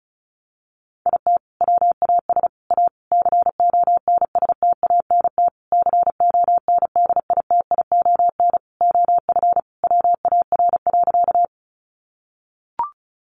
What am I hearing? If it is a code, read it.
Morse code, heard as a keyed tone, about 34 wpm, 700 Hz. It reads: IT WAS A CONSTANT CONDITION OF WAR.
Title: Sentences from Top 1500 Words 35wpm